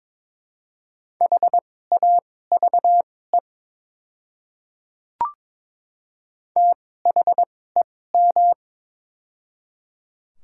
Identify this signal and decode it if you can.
Morse code, heard as a keyed tone, about 21 wpm, spaced as for 16 wpm, 700 Hz. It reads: HAVE THEM
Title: 2x Character Spacing - Top 100 Words - 22wpm